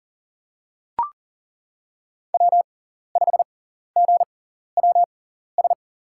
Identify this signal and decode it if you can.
Morse code, heard as a keyed tone, about 40 wpm, spaced as for 14 wpm, 700 Hz. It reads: W5GWS
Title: USA Call Signs - 5 Characters - 6x Character Spacing - 40wpm